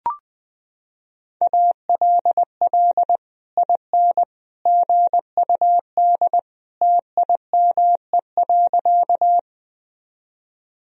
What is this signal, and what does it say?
Morse code, heard as a keyed tone, about 20 wpm, 700 Hz. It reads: ALL IN GUD TIME.
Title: Sentences from Top 200 Words and Abbreviations 20wpm